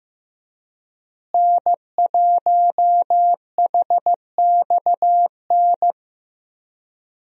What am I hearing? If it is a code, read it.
Morse code, heard as a keyed tone, about 15 wpm, 700 Hz. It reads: N1HXN
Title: USA Call Signs - 5 Characters - Repeated 3x - 15wpm